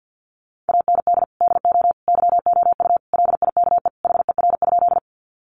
Morse code, heard as a keyed tone, about 50 wpm, 700 Hz. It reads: AND DO YOU LIKE HER?
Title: Sentences from Top 100 Words 50wpm